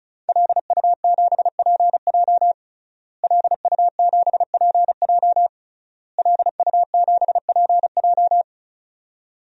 Morse code, heard as a keyed tone, about 34 wpm, 700 Hz. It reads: LU7PJ LU7PJ LU7PJ
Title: DXCC-100 Call Signs - 5 Characters - Repeated 3x - 35wpm